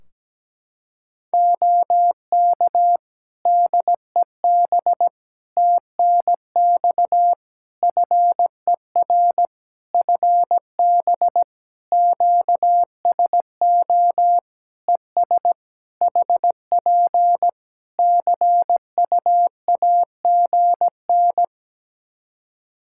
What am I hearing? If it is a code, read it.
Morse code, heard as a keyed tone, about 17 wpm, 700 Hz. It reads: OK DEB TNX FER FB QSO ES HP CU AGN
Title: Practice QSO - Exchange 4 - Ending - 17wpm